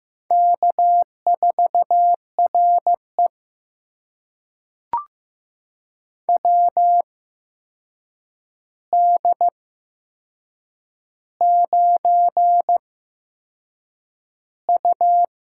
Morse code, heard as a keyed tone, about 15 wpm, spaced as for 6 wpm, 700 Hz. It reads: K4RE WD9U
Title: USA Call Signs - 4 Characters - 8x Character Spacing - 15wpm